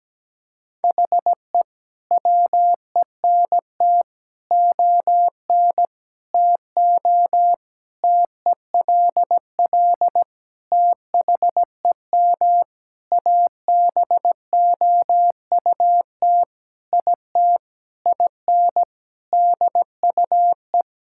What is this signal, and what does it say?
Morse code, heard as a keyed tone, about 17 wpm, 700 Hz. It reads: HE WENT ON TO TELL THEM ABOUT IT IN DUE
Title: Sentences from Top 1000 Words - Review Part 6 - 17wpm